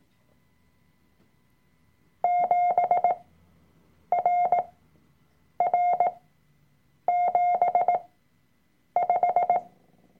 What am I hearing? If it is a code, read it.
Morse code, heard as a keyed tone, about 18 wpm, spaced as for 7 wpm, 710 Hz. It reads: 7RR75
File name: Workshop Dateien / CW-Online-Kurs / Lektion 003